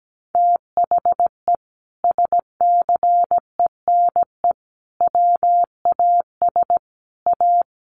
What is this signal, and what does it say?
Morse code, heard as a keyed tone, about 17 wpm, 700 Hz. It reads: THE SCENE WAS A
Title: Sentences from Top 1100 Words 17wpm